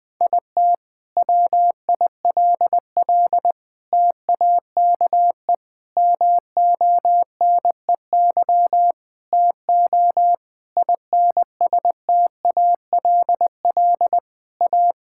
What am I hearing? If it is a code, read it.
Morse code, heard as a keyed tone, about 20 wpm, 700 Hz. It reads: IT WILL TAKE MONEY TO INSTALL A
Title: Sentences from Top 1600 Words 20wpm